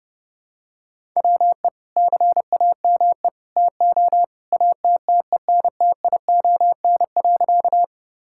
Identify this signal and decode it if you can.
Morse code, heard as a keyed tone, about 30 wpm, 700 Hz. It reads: WE CAME TO ATTENTION.
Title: Sentences from Top 1000 Words - Review Part 6 - 30wpm